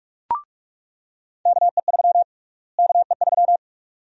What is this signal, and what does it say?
Morse code, heard as a keyed tone, about 46 wpm, 700 Hz. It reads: KE3 KE3
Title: USA Call Sign Prefixes - Repeated 2x - 45wpm